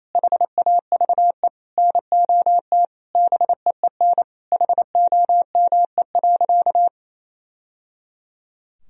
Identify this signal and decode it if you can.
Morse code, heard as a keyed tone, about 28 wpm, 700 Hz. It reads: HAVE NOT BEEN HOME.
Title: Sentences from Top 1000 Words - Review Part 1 - 28wpm